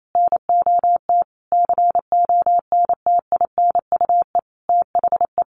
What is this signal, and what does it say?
Morse code, heard as a keyed tone, about 28 wpm, 700 Hz. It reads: NOT CONTINUE THE
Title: Sentences from Top 1200 Words 28wpm